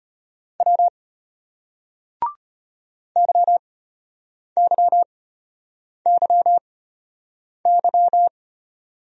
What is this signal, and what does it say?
Morse code, heard as a keyed tone, about 33 wpm, spaced as for 8 wpm, 700 Hz. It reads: W YYYY